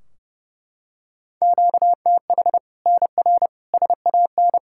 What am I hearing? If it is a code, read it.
Morse code, heard as a keyed tone, about 32 wpm, 700 Hz. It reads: QTH NR SAN
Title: Practice QSO - Exchange 1 - Beginning - 30wpm